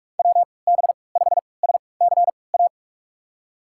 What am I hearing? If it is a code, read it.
Morse code, heard as a keyed tone, about 46 wpm, spaced as for 26 wpm, 700 Hz. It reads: WB5SCA
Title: USA Call Signs - 6 Characters - 3x Character Spacing - 45wpm